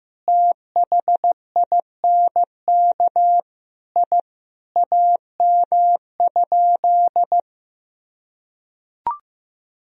Morse code, heard as a keyed tone, about 15 wpm, 700 Hz. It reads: THINK I AM?
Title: Sentences from Top 1000 Words - Review Part 3 - 15wpm